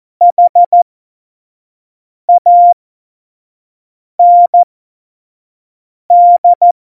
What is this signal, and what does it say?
Morse code, heard as a keyed tone, about 14 wpm, spaced as for 5 wpm, 700 Hz. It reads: HAND
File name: Morsekurs / Witze / Witz 14. August 2018, 25 F70